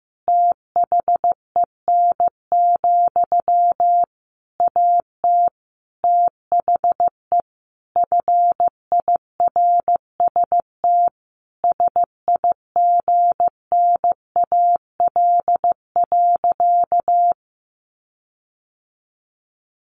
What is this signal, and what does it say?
Morse code, heard as a keyed tone, about 15 wpm, 700 Hz. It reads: THEN, AT THE FIRST SIGNAL.
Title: Sentences from Top 1400 Words 15wpm